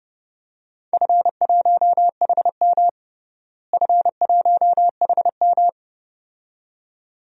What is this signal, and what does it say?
Morse code, heard as a keyed tone, about 32 wpm, 700 Hz. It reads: F1HM F1HM
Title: DXCC-100 Call Signs - 4 Characters - Repeated 2x - 30wpm